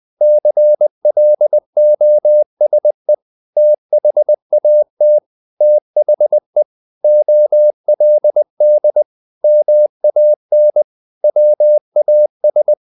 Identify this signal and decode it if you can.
Morse code, heard as a keyed tone, about 20 wpm, 600 Hz. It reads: CLOSE THAT THE OLD MAN WAS
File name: the old man and the sea / the old man and the sea part 1 20wpm